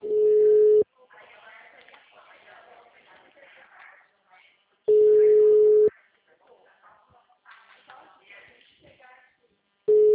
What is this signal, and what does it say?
Ringback tone (Europe, 425 Hz), with speech.